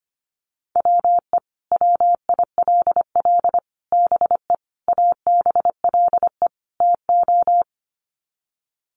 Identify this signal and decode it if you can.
Morse code, heard as a keyed tone, about 25 wpm, 700 Hz. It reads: WE WILL BE ABLE TO
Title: Sets of 5 Words Encore 25wpm